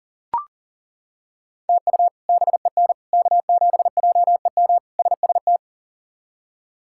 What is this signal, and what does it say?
Morse code, heard as a keyed tone, about 40 wpm, 700 Hz. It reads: TU BEN K7JEM SST
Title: K1USN SST QSO Practice - 40wpm